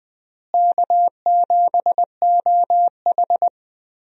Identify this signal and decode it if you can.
Morse code, heard as a keyed tone, about 20 wpm, 690 Hz. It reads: K7OH
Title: USA Call Signs - 4 Characters - Repeated 3x - 20wpm